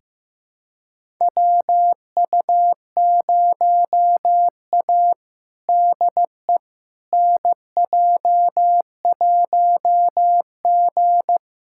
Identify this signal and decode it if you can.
Morse code, heard as a keyed tone, about 15 wpm, 700 Hz. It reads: WU0A DE NJ1G
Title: USA Callsign DE Callsign - 4 Characters - 15wpm